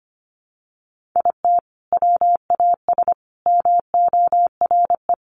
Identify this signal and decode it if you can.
Morse code, heard as a keyed tone, about 25 wpm, 700 Hz. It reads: IT WAS MORE
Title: Sentences from Top 1400 Words 25wpm